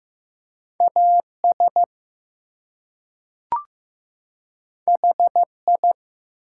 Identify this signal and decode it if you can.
Morse code, heard as a keyed tone, about 15 wpm, 700 Hz. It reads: AS HI
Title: Two Letter Words - 15wpm